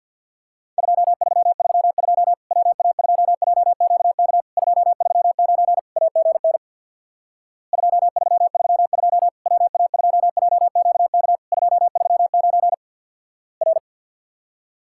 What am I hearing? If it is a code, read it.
Morse code, heard as a keyed tone, about 50 wpm, 700 Hz. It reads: 2332 WA2JQK 239 AGN 2332 WA2JQK 239 R
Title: Worked All Europe DX Contest - QTC Exchange with Repeats - 50wpm